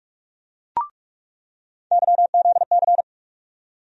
Morse code, heard as a keyed tone, about 46 wpm, 700 Hz. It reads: YZC
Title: USA Call Sign Postfixes - 3 Letters 45wpm